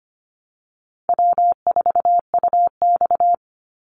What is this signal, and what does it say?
Morse code, heard as a keyed tone, about 25 wpm, 700 Hz. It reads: W4UX
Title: Call Signs - 25wpm